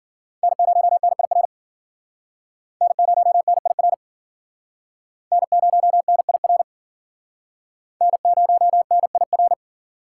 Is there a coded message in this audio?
Yes: Morse code from 0.4 to 9.5 s, heard as a keyed tone, about 50 wpm, 700 Hz. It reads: N0NIR N0NIR N0NIR N0NIR